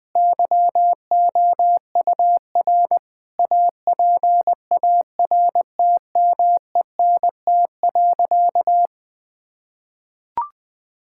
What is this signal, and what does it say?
Morse code, heard as a keyed tone, about 19 wpm, 700 Hz. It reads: YOUR APARTMENT.
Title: Sentences from Top 1000 Words - Review Part 6 - 20wpm